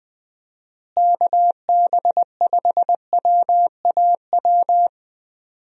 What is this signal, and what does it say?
Morse code, heard as a keyed tone, about 20 wpm, 700 Hz. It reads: KB5WAW
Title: USA Call Signs - 6 Characters - Repeated 3x - 20wpm